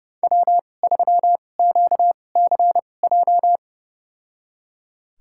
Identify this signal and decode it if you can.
Morse code, heard as a keyed tone, about 29 wpm, spaced as for 21 wpm, 700 Hz. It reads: W3QCJ
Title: USA Call Signs - 5 Characters - 2x Character Spacing - 30wpm